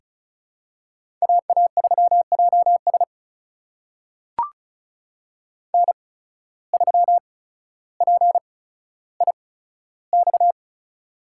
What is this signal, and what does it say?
Morse code, heard as a keyed tone, about 36 wpm, spaced as for 15 wpm, 700 Hz. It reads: AA3JS N3PIX
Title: USA Call Signs - 5 Characters - 8x Character Spacing - 35wpm